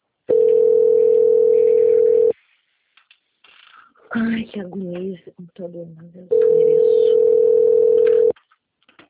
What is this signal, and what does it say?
Ringback tone (North America), 440 + 480 Hz, with speech.
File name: out- - - - - .4717